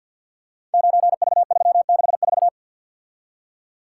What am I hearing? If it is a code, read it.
Morse code, heard as a keyed tone, about 51 wpm, 700 Hz. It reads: 9V364
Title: Code Groups - 5 Letter:Numbers - 50wpm